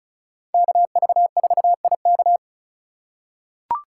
Morse code, heard as a keyed tone, about 34 wpm, 700 Hz. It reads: KV4IK
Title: USA Call Signs - 6x Character Spacing - 35wpm